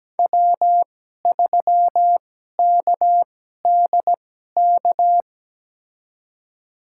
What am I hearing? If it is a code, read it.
Morse code, heard as a keyed tone, about 17 wpm, spaced as for 12 wpm, 700 Hz. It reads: W3KDK
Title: USA Call Signs - 5 Characters - 2x Character Spacing - 17wpm